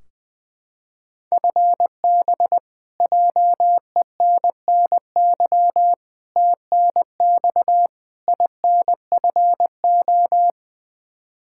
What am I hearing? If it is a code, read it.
Morse code, heard as a keyed tone, about 21 wpm, 700 Hz. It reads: FB JENNY TNX INFO
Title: Practice QSO - Exchange 3 - More Info - 20wpm